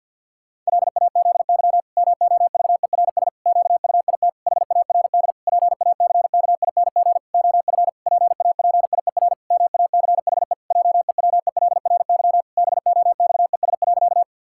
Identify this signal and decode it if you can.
Morse code, heard as a keyed tone, about 50 wpm, 700 Hz. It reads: LAZY MOVERS QUIT HARD PACKING OF PAPIER MACHE JEWELRY BOXES.
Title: Pangram Challenge v2 - 50wpm